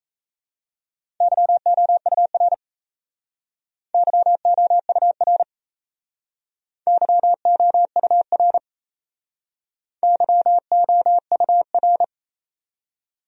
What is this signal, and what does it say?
Morse code, heard as a keyed tone, about 38 wpm, 700 Hz. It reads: YOUR YOUR YOUR YOUR